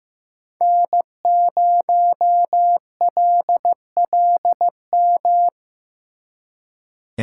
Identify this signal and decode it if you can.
Morse code, heard as a keyed tone, about 17 wpm, 700 Hz. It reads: N0LLM
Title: Speed Racing - USA 1x3 Call Signs 15wpm